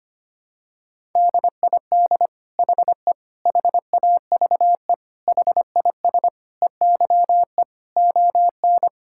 Morse code, heard as a keyed tone, about 25 wpm, 700 Hz. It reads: DID HE HAVE HIS EYE ON
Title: Sentences from Top 1000 Words - Review Part 5 - 25wpm